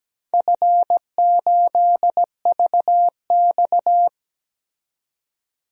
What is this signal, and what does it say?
Morse code, heard as a keyed tone, about 17 wpm, 700 Hz. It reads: F8VX
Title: DXCC-100 Call Signs - 4 Characters - 17wpm